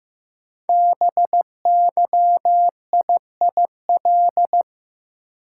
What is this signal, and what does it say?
Morse code, heard as a keyed tone, about 15 wpm, 700 Hz. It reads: BYIIL
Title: Code Groups - 5 Letters - 15wpm